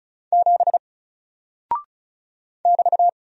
Morse code, heard as a keyed tone, about 36 wpm, 700 Hz. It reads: 7 <BT>
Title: Single Character (Letter:Number:Punctuation:Prosign) 35wpm